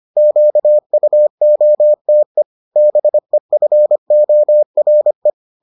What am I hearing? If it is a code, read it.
Morse code, heard as a keyed tone, about 25 wpm, 600 Hz. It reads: QUOTE BEFORE